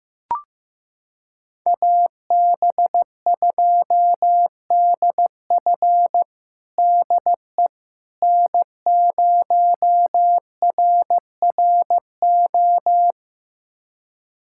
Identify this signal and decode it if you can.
Morse code, heard as a keyed tone, about 15 wpm, 700 Hz. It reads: AB2DF DE N0RRO
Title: USA Callsign DE Callsign 15wpm